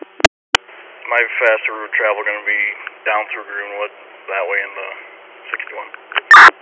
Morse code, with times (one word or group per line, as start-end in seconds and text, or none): none